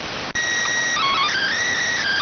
SSTV (slow-scan television), PD 290.